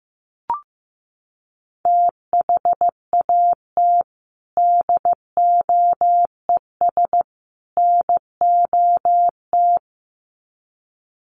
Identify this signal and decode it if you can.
Morse code, heard as a keyed tone, about 15 wpm, 700 Hz. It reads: THAT DOES NOT
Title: Sets of 3 Words 15wpm